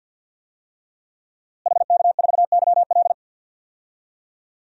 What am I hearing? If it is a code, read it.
Morse code, heard as a keyed tone, about 47 wpm, 700 Hz. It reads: HK4YL
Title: DXCC-100 Call Signs - 5 Characters - Repeated 3x - 50wpm